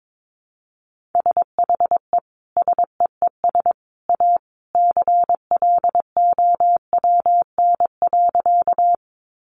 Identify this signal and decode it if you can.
Morse code, heard as a keyed tone, about 22 wpm, 700 Hz. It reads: SHE SEES A CLOWN.
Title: Easy Phrases 22wpm